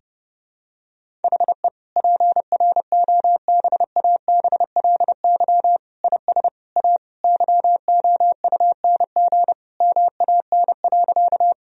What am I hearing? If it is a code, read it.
Morse code, heard as a keyed tone, about 32 wpm, 700 Hz. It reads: HE PROBABLY IS A YOUNG MAN.
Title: Sentences from Top 1000 Words - Review Part 4 - 30wpm